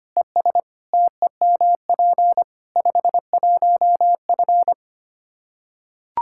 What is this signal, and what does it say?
Morse code, heard as a keyed tone, about 25 wpm, 700 Hz. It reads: ES TEMP 51F
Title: Practice QSO - Exchange 2 - Info - 25wpm